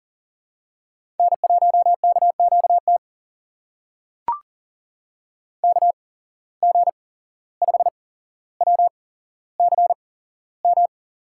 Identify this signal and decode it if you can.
Morse code, heard as a keyed tone, about 37 wpm, spaced as for 16 wpm, 700 Hz. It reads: N1KQT KG5WCM
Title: USA Call Signs - 8x Character Spacing - 40wpm